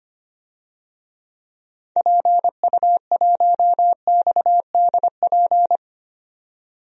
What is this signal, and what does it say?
Morse code, heard as a keyed tone, about 25 wpm, 700 Hz. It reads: PU1XDP
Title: DXCC-100 Call Signs - 6 Characters - 25wpm